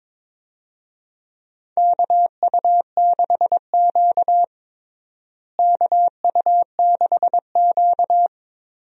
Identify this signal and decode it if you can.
Morse code, heard as a keyed tone, about 21 wpm, 700 Hz. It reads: KU6Q KU6Q
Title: USA Call Signs - 4 Characters - Repeated 3x - 22wpm